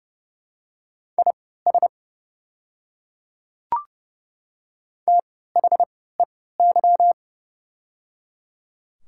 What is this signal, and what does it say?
Morse code, heard as a keyed tone, about 30 wpm, spaced as for 17 wpm, 700 Hz. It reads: IS THEY
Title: 3x Character Spacing - Top 500 Words - 30wpm